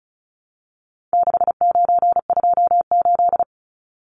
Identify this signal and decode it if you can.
Morse code, heard as a keyed tone, about 34 wpm, 700 Hz. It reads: 6928
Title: Numbers - 4-Digit 35wpm